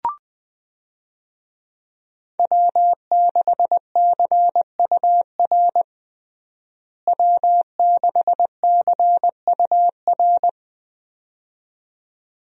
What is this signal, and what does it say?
Morse code, heard as a keyed tone, about 20 wpm, 700 Hz. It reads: W6CUR W6CUR
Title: USA Call Signs - 5 Characters - Repeated 2x - 20wpm